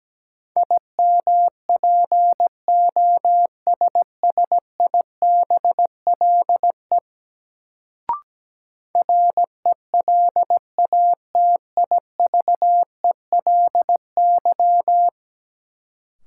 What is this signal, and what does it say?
Morse code, heard as a keyed tone, about 17 wpm, 700 Hz. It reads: IMPOSSIBLE RELATIVELY
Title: Sentences from Top 1100 Words - New Word Drill - 17wpm